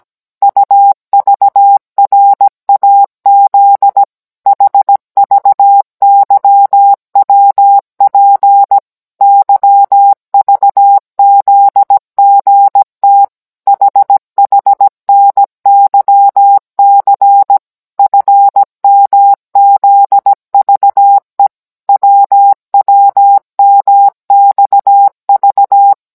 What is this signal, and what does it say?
Morse code, heard as a keyed tone, about 17 wpm, 800 Hz. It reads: UVRAZ HVYWP YVZGT HHNYC FMZVE WWMXV